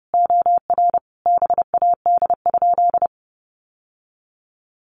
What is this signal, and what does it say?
Morse code, heard as a keyed tone, about 30 wpm, 700 Hz. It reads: OR BAD?
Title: Sentences from Top 800 Words 30wpm